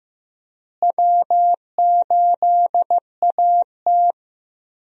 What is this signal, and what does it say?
Morse code, heard as a keyed tone, about 15 wpm, 700 Hz. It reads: W8AT
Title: USA Call Signs - 4 Characters - Repeated 2x - 15wpm